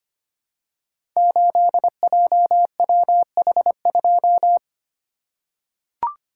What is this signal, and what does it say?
Morse code, heard as a keyed tone, about 25 wpm, 700 Hz. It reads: 8JWH2
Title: Code Groups - 5 Letter:Numbers - 25wpm